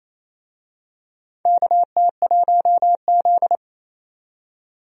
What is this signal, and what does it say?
Morse code, heard as a keyed tone, about 28 wpm, 700 Hz. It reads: KT1Z
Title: USA Call Signs - 4 Characters - 28wpm